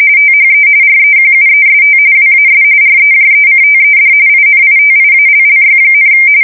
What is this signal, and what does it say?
RTTY, about 45.45 baud. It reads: THE REQUEST WAS REJECTED INITIALLY,
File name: fsk170 / usnavy3